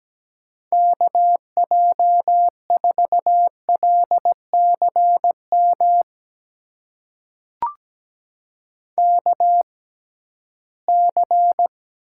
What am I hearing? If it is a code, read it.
Morse code, heard as a keyed tone, about 17 wpm, spaced as for 13 wpm, 700 Hz. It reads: KJ4LCM KC
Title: USA Call Signs - 6 Characters - 6x Character Spacing - 17wpm